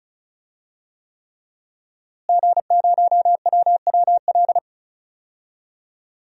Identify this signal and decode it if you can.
Morse code, heard as a keyed tone, about 36 wpm, 700 Hz. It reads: G0WWL
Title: DXCC-100 Call Signs - 5 Characters - Repeated 3x - 35wpm